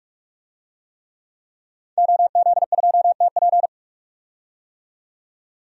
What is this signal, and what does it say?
Morse code, heard as a keyed tone, about 46 wpm, 700 Hz. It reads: OZ2TP
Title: DXCC-100 Call Signs - 5 Characters - 45wpm